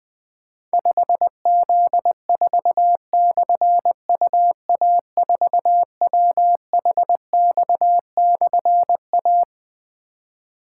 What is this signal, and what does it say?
Morse code, heard as a keyed tone, about 20 wpm, 700 Hz. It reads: 5Z4/UA4WHX/A
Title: Difficult Call Signs 20wpm